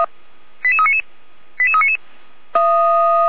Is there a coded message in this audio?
Yes: five-tone selective call (ZVEI) from 0.6 to 2.0 s, heard as quick runs of five tones, tones of about 80 ms. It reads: 88399 88399